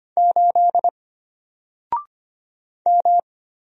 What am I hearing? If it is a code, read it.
Morse code, heard as a keyed tone, about 25 wpm, 700 Hz. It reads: 8 M
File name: Single-Letter-Number-25wpm